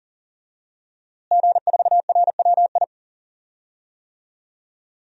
Morse code, heard as a keyed tone, about 40 wpm, 700 Hz. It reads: G4RWI
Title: DXCC-100 Call Signs - 5 Characters - Repeated 2x - 40wpm